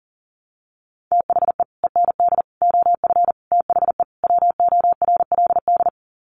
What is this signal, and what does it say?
Morse code, heard as a keyed tone, about 40 wpm, 700 Hz. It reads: THE END OF THE WORLD
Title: Sets of 5 Words 40wpm